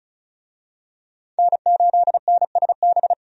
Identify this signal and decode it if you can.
Morse code, heard as a keyed tone, about 36 wpm, 700 Hz. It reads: N8NSB
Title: USA Call Signs - 5 Characters - 35wpm